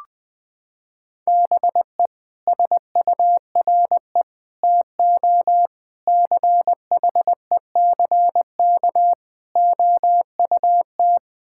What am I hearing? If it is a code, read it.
Morse code, heard as a keyed tone, about 21 wpm, 700 Hz. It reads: BE SURE TO CHECK OUT
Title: Sets of 6 Words - 20wpm